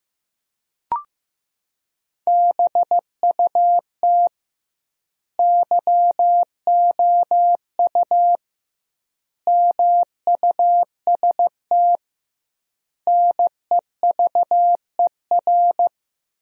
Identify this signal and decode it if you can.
Morse code, heard as a keyed tone, about 15 wpm, 700 Hz. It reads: BUT YOU MUST NEVER
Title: 2x Word-Spacing - Sentences from Top 300 Words - 15wpm